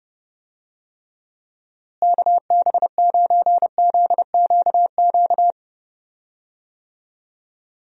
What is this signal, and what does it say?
Morse code, heard as a keyed tone, about 31 wpm, 700 Hz. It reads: KB9ZQQ
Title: USA Call Signs - 6 Characters - Repeated 2x - 30wpm